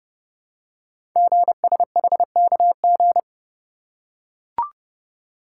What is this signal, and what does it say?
Morse code, heard as a keyed tone, about 30 wpm, 700 Hz. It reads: GSHKG
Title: Code Groups - 5 Letters - 30wpm